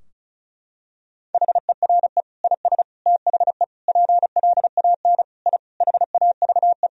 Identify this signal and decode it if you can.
Morse code, heard as a keyed tone, about 34 wpm, 700 Hz. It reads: HERE IS THE PLAN I HAVE
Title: Sentences from Top 1000 Words - Review Part 2 - 35wpm